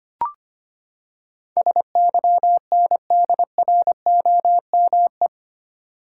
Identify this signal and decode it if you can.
Morse code, heard as a keyed tone, about 25 wpm, 700 Hz. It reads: SYNDROME
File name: Eight Letter Words - 25wpm